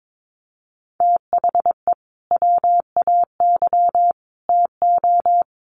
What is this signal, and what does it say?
Morse code, heard as a keyed tone, about 22 wpm, 700 Hz. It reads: THE WAY TO
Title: Sets of 3 Words 22wpm